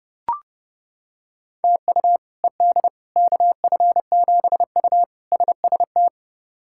Morse code, heard as a keyed tone, about 32 wpm, 700 Hz. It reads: TU ED KF7U SST
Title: K1USN SST QSO Practice - 30wpm